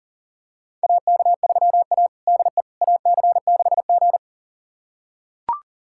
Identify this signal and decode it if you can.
Morse code, heard as a keyed tone, about 40 wpm, 700 Hz. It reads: AK3A DE AC6G
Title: USA Callsign DE Callsign - 4 Characters - 40wpm